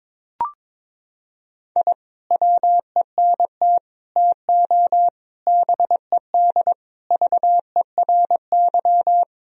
Morse code, heard as a keyed tone, about 22 wpm, 700 Hz. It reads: I WENT TO BED VERY
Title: Sentences from Top 1000 Words - Review Part 2 - 22wpm